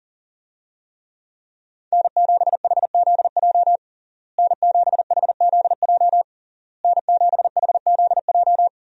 Morse code, heard as a keyed tone, about 40 wpm, 700 Hz. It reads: N7HZJ N7HZJ N7HZJ
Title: USA Call Signs - 5 Characters - Repeated 3x - 40wpm